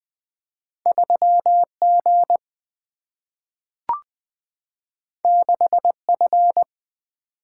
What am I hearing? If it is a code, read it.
Morse code, heard as a keyed tone, about 20 wpm, 700 Hz. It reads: 3G 6F
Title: Single-Letter Single-Number 20wpm